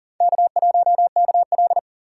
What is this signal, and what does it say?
Morse code, heard as a keyed tone, about 55 wpm, spaced as for 47 wpm, 700 Hz. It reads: K1KL
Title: Speed Racing - USA 1x2 Call Signs 40wpm